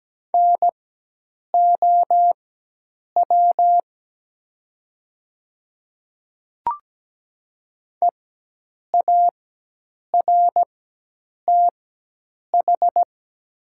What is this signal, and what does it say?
Morse code, heard as a keyed tone, about 17 wpm, spaced as for 8 wpm, 700 Hz. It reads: NOW EARTH
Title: 4x Character Spacing - Top 300 Words - 17wpm